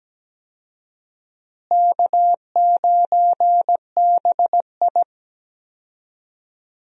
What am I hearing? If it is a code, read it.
Morse code, heard as a keyed tone, about 17 wpm, 700 Hz. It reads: K9BI
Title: USA Call Signs - 4 Characters - Repeated 2x - 17wpm